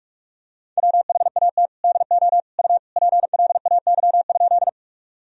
Morse code, heard as a keyed tone, about 46 wpm, 700 Hz. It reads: WHAT DO U PLAY?
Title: Sentences from Top 200 Words and Abbreviations 45wpm